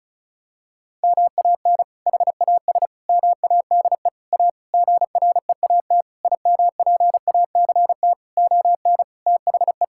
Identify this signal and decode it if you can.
Morse code, heard as a keyed tone, about 34 wpm, 700 Hz. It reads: MAN HAS MADE A GREAT IMPACT ON THE
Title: Sentences from Top 1000 Words - Review Part 6 - 35wpm